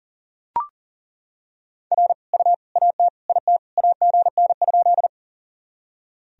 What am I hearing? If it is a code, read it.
Morse code, heard as a keyed tone, about 37 wpm, 700 Hz. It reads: R U AT IT AGN?
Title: Sentences from Top 200 Words and Abbreviations 40wpm